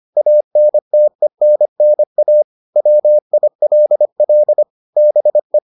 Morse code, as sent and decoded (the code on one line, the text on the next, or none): .- -. - . -. -. .- / .-- .. .-.. .-.. / -... .
ANTENNA WILL BE